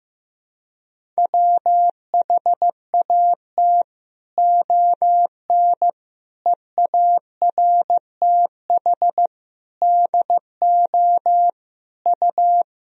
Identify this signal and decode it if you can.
Morse code, heard as a keyed tone, about 15 wpm, 700 Hz. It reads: WHAT ON EARTH DO U
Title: Sentences from Top 200 Words and Abbreviations 15wpm